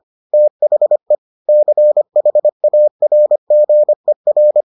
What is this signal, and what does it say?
Morse code, heard as a keyed tone, about 25 wpm, 600 Hz. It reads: THE CHARGER